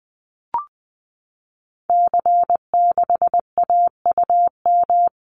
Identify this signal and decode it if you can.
Morse code, heard as a keyed tone, about 20 wpm, 700 Hz. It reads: C6AUM
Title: Call Signs Repeated 3x - 20wpm